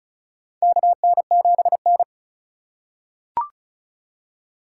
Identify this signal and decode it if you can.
Morse code, heard as a keyed tone, about 36 wpm, 700 Hz. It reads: KN7N
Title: USA Call Signs - 4x Character Spacing - 35wpm